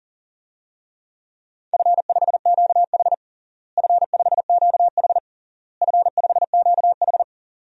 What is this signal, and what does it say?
Morse code, heard as a keyed tone, about 40 wpm, 700 Hz. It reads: F5QH F5QH F5QH
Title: DXCC-100 Call Signs - 4 Characters - Repeated 3x - 40wpm